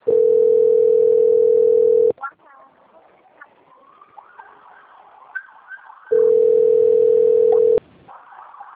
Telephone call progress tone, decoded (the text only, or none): ringback tone